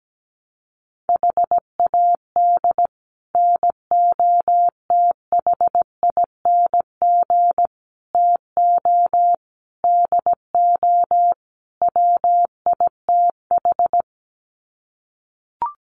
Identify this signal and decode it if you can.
Morse code, heard as a keyed tone, about 17 wpm, 700 Hz. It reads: HAD NOTHING TO DO WITH
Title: Sets of 5 Words 17wpm